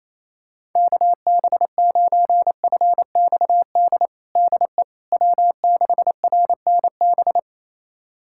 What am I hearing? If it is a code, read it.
Morse code, heard as a keyed tone, about 28 wpm, 700 Hz. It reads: KB9FXD DE W6RNB
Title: USA Callsign DE Callsign 28wpm